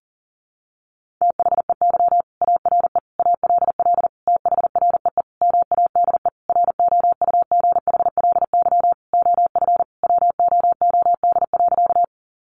Morse code, heard as a keyed tone, about 40 wpm, 700 Hz. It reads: THEY ARE ALL THREE MADE ROUGHLY OF WOOD.
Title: Sentences from Top 1500 Words 40wpm